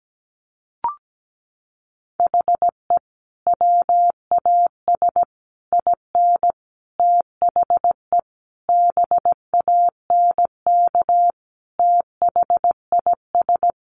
Morse code, heard as a keyed tone, about 17 wpm, 700 Hz. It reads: HE WAS IN THE BANK THIS
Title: Sentences from Top 900 Words 17wpm